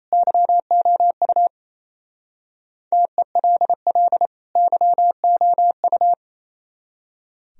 Morse code, heard as a keyed tone, about 35 wpm, 700 Hz. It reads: YOU TELL YOU
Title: Speed-Racing - Sets of 2 Words 28wpm